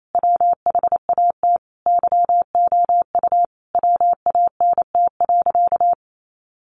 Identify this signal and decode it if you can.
Morse code, heard as a keyed tone, about 28 wpm, 700 Hz. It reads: WHAT YOU WANT.
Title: Sentences from Top 1100 Words 28wpm